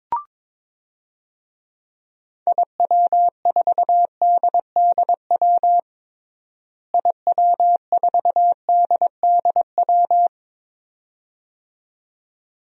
Morse code, heard as a keyed tone, about 22 wpm, 700 Hz. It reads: IW4DDW IW4DDW
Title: DXCC-100 Call Signs - 6 Characters - Repeated 2x - 22wpm